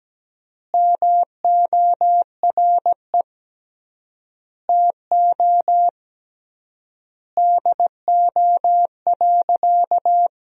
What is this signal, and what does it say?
Morse code, heard as a keyed tone, about 17 wpm, 700 Hz. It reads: MORE TO DO.
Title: 3x Word-Spacing - Sentences from Top 100 Words - 17wpm